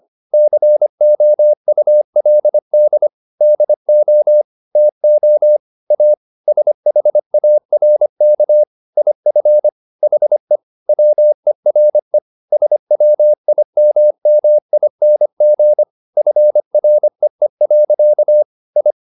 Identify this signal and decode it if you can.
Morse code, heard as a keyed tone, about 25 wpm, 600 Hz. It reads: COULD DO TO A SHARK IF HE WERE SWIMMING FREE. I